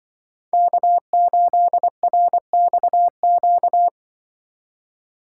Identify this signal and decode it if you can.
Morse code, heard as a keyed tone, about 24 wpm, 700 Hz. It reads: K8RXQ
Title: Speed Racing - USA 1x3 Call Signs 20wpm